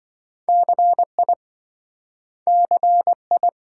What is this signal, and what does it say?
Morse code, heard as a keyed tone, about 27 wpm, 700 Hz. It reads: CI CI